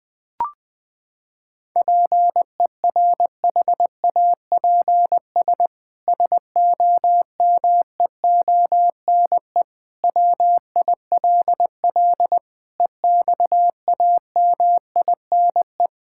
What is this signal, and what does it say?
Morse code, heard as a keyed tone, about 21 wpm, 700 Hz. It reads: PERHAPS SOMEONE WILL EXAMINE
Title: Sentences from Top 1600 Words 20wpm